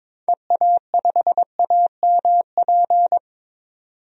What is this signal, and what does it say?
Morse code, heard as a keyed tone, about 21 wpm, 700 Hz. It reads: EA5AMP
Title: DXCC-100 Call Signs - 6 Characters - 22wpm